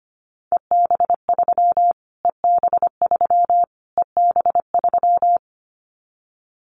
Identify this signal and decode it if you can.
Morse code, heard as a keyed tone, about 25 wpm, 700 Hz. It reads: EB3 EB3 EB3
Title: Callsign Prefixes for Top 25 DXCC - Repeated 3x - 25wpm